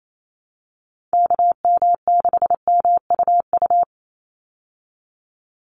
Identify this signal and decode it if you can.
Morse code, heard as a keyed tone, about 28 wpm, 700 Hz. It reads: KM6MUU
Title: USA Call Signs 28wpm